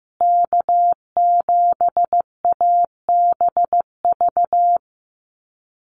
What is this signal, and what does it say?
Morse code, heard as a keyed tone, about 15 wpm, 700 Hz. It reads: K7ABV
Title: Call Signs Repeated 3x - 15wpm